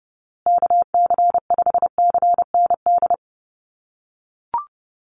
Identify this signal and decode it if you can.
Morse code, heard as a keyed tone, about 30 wpm, 700 Hz. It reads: KC5CND
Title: USA Call Signs Repeated 2x 30wpm